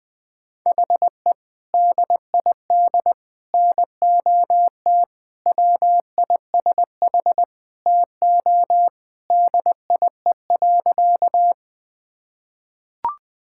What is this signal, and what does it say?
Morse code, heard as a keyed tone, about 20 wpm, 700 Hz. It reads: HE DID NOT WISH TO DIE.
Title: Sentences from Top 1600 Words 20wpm